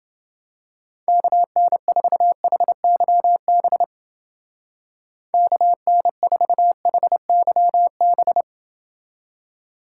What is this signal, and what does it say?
Morse code, heard as a keyed tone, about 29 wpm, 700 Hz. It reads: KN4HYB KN4HYB